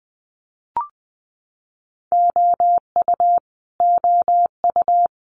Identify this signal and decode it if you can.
Morse code, heard as a keyed tone, about 20 wpm, 700 Hz. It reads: OU OU
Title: Multi-letter Phonemes Repeated 3x 20wpm